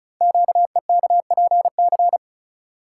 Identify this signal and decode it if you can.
Morse code, heard as a keyed tone, about 36 wpm, 690 Hz. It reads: QEKPC
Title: Code Groups - 5 Letters - 35wpm